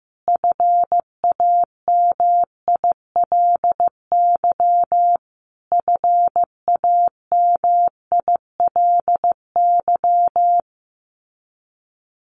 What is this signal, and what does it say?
Morse code, heard as a keyed tone, about 15 wpm, 700 Hz. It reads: FAMILY FAMILY
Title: Top 500 Words Repeated 15wpm